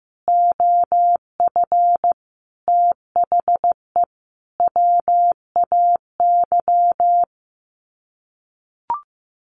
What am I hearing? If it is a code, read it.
Morse code, heard as a keyed tone, about 15 wpm, 700 Hz. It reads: OF THE WAY
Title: Sets of 4 Words 15wpm